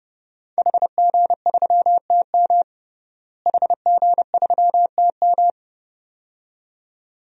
Morse code, heard as a keyed tone, about 30 wpm, 700 Hz. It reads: HG3TM HG3TM
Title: DXCC-100 Call Signs - 5 Characters - Repeated 2x - 30wpm